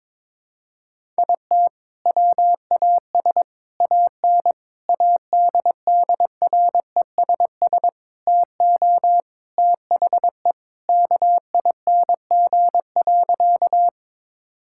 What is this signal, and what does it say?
Morse code, heard as a keyed tone, about 21 wpm, 700 Hz. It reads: IT WAS AN ADDRESS TO THE KING.
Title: Sentences from Top 1000 Words - Review Part 6 - 22wpm